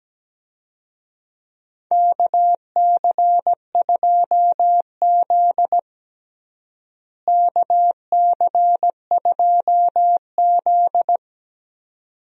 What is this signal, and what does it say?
Morse code, heard as a keyed tone, about 17 wpm, 700 Hz. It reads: KC2Z KC2Z
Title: USA Call Signs - 4 Characters - Repeated 2x - 17wpm